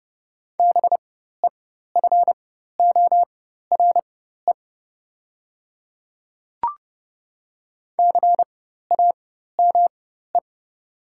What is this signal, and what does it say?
Morse code, heard as a keyed tone, about 30 wpm, spaced as for 14 wpm, 700 Hz. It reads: BEFORE CAME
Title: 4x Character Spacing - Top 200 Words - 30wpm